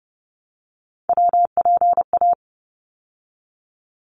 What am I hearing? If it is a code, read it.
Morse code, heard as a keyed tone, about 30 wpm, 700 Hz. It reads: WPA
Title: ARRL Sections 30wpm